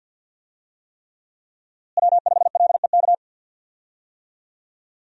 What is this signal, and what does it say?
Morse code, heard as a keyed tone, about 46 wpm, 700 Hz. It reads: W5LEK
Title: USA Call Signs - 5 Characters - Repeated 3x - 50wpm